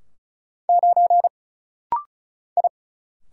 Morse code, heard as a keyed tone, about 35 wpm, 700 Hz. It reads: 9 I
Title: Single Letter-Number Alt - Rapid-Fire 35wpm